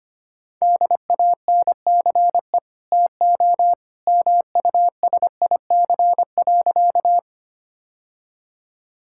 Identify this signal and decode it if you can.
Morse code, heard as a keyed tone, about 25 wpm, 700 Hz. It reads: DANCE TO MUSIC.
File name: Easy Sentences with Hints - 25wpm